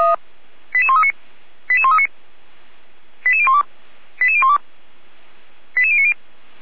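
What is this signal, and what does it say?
Five-tone selective call (ZVEI), heard as quick runs of five tones, tones of about 70 ms. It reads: 88139 88139 88012 88012 88090
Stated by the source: ZVEI five-tone selective call received on SDR, from the feuerwehr-orlamuende.org fire brigade server.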